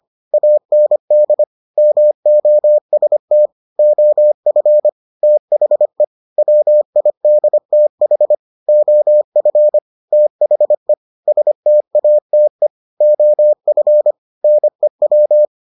Morse code, heard as a keyed tone, about 25 wpm, 600 Hz. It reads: AND MOST OF THE WIDTH OF THE STATE OF NEW